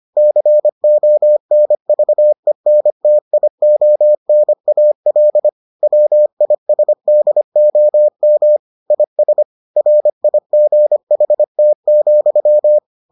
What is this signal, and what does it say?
Morse code, heard as a keyed tone, about 25 wpm, 600 Hz. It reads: CONVENTIONAL WISDOM IS RIGHT,